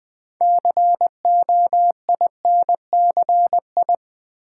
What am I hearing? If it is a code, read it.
Morse code, heard as a keyed tone, about 21 wpm, 700 Hz. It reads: COINCI
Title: Build A Word - 2 Letters At A Time - 8-Letter Words - 20wpm